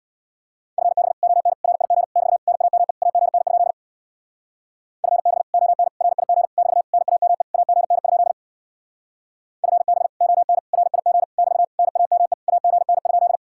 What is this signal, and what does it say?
Morse code, heard as a keyed tone, about 64 wpm, 700 Hz. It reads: FB ON RIG <BT> NAME AGN? FB ON RIG <BT> NAME AGN? FB ON RIG <BT> NAME AGN?